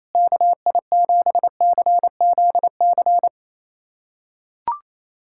Morse code, heard as a keyed tone, about 28 wpm, 700 Hz. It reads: KI7CZC
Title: USA Callsign DE Callsign - 6 Characters - 28wpm